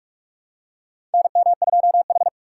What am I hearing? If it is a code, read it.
Morse code, heard as a keyed tone, about 41 wpm, 700 Hz. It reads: NM2H
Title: USA Call Signs - 3x Character Spacing - 45wpm